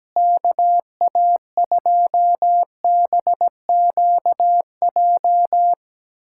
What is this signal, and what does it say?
Morse code, heard as a keyed tone, about 17 wpm, 700 Hz. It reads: KA2BQJ
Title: USA Callsign DE Callsign - 6 Characters - 17wpm